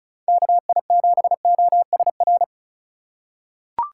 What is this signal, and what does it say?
Morse code, heard as a keyed tone, about 34 wpm, 700 Hz. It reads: KI7OSR
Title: USA Call Signs - 6 Characters - 4x Character Spacing - 35wpm